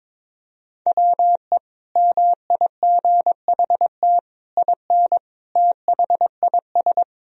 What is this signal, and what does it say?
Morse code, heard as a keyed tone, about 22 wpm, 700 Hz. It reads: WE MIGHT IN THIS
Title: Sentences from Top 1000 Words - Review Part 1 - 22wpm